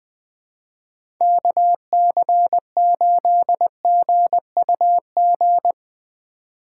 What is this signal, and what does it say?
Morse code, heard as a keyed tone, about 20 wpm, 700 Hz. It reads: KC8GUG
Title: USA Call Signs - 6 Characters - 2x Character Spacing - 20wpm